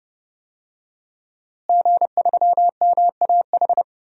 Morse code, heard as a keyed tone, about 30 wpm, 700 Hz. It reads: G3MAH
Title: DXCC-100 Call Signs - 5 Characters - Repeated 2x - 30wpm